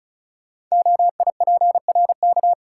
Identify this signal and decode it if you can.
Morse code, heard as a keyed tone, about 36 wpm, 690 Hz. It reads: OIPRK
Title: Code Groups - 5 Letters - 35wpm